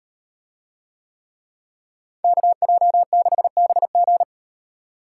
Morse code, heard as a keyed tone, about 38 wpm, 700 Hz. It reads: KJ6BG